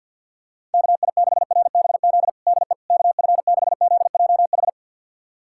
Morse code, heard as a keyed tone, about 53 wpm, 700 Hz. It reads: KI6RBZ DE KF6ZJH
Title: USA Callsign DE Callsign - 6 Characters - 50wpm